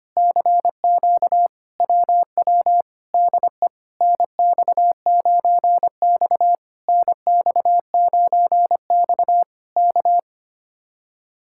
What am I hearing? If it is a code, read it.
Morse code, heard as a keyed tone, about 24 wpm, 700 Hz. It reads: CQ WW DE NX9X NX9X K
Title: Speed-Racing Common QSO Phrases - 25wpm